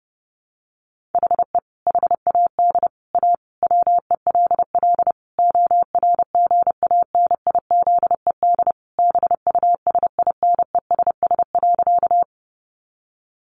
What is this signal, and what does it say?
Morse code, heard as a keyed tone, about 30 wpm, 700 Hz. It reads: HE HAD A WELL ORGANIZED BUSINESS.
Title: Sentences from Top 1300 Words 30wpm